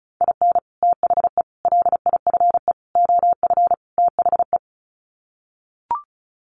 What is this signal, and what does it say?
Morse code, heard as a keyed tone, about 34 wpm, 700 Hz. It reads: IN THE LIFE OF THE
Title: Sets of 5 Words Encore2 35wpm